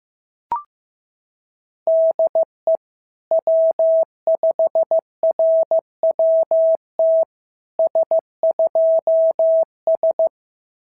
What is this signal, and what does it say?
Morse code, heard as a keyed tone, about 16 wpm, 650 Hz. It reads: DE W5RWT S2S
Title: SOTA QSO Practice 15wpm